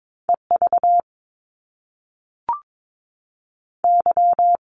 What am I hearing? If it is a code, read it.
Morse code, heard as a keyed tone, about 22 wpm, 700 Hz. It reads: E V Y
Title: Top Two Letter Combinations in Words 22wpm